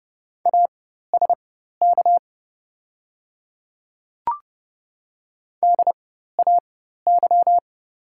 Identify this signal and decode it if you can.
Morse code, heard as a keyed tone, about 30 wpm, spaced as for 14 wpm, 700 Hz. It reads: ASK DAY
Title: 4x Character Spacing - Top 200 Words - 30wpm